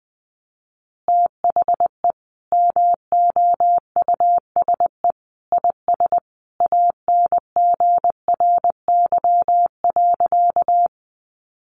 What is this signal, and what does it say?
Morse code, heard as a keyed tone, about 20 wpm, 700 Hz. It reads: THE MOUSE IS ANGRY.
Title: Easy Phrases 20wpm